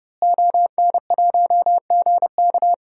Morse code, heard as a keyed tone, about 30 wpm, 700 Hz. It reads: ON1GK
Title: DXCC-100 Call Signs - 5 Characters - Repeated 2x - 30wpm